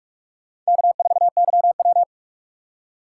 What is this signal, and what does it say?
Morse code, heard as a keyed tone, about 47 wpm, 700 Hz. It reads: K4YW
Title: USA Call Signs - 4 Characters - 4x Character Spacing - 45wpm